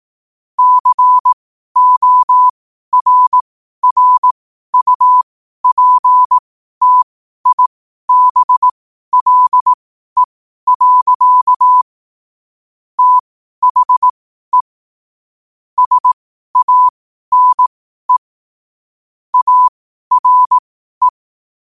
Morse code, heard as a keyed tone, about 18 wpm, spaced as for 12 wpm, 1000 Hz. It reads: CORRUPTIBLE. THE SANE ARE
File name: qotdfiles / 12wpm